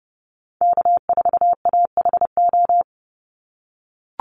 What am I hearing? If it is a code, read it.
Morse code, heard as a keyed tone, about 30 wpm, 700 Hz. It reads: K4AHO
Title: Call Signs - 30wpm